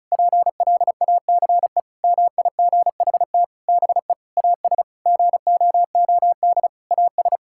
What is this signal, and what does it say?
Morse code, heard as a keyed tone, about 36 wpm, 700 Hz. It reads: PLACE MIGHT BE AS GOOD AS
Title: Sentences from Top 1600 Words 35wpm